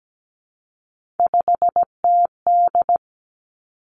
Morse code, heard as a keyed tone, about 17 wpm, 700 Hz. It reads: 5TD
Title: Three Challenging Letter-Number Combinations 17wpm